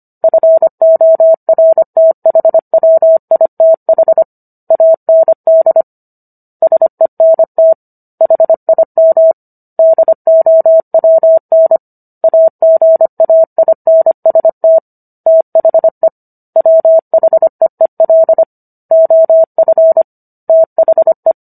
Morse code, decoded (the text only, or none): FORTHWITH AND SENT HIM DOWN AGAINST THE WHEEL OF THE